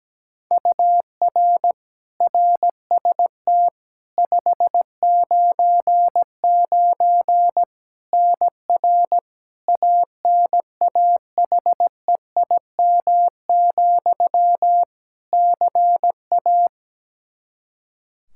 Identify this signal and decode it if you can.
Morse code, heard as a keyed tone, about 17 wpm, 700 Hz. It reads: UR RST 599 NR ANAHEIM, CA
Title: Practice QSO - Exchange 1 - Beginning - 17wpm